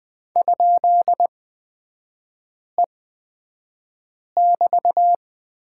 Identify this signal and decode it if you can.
Morse code, heard as a keyed tone, about 20 wpm, spaced as for 5 wpm, 700 Hz. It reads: ?E<BT>
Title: 224 Lesson 36e - BT - Three Characters 20wpm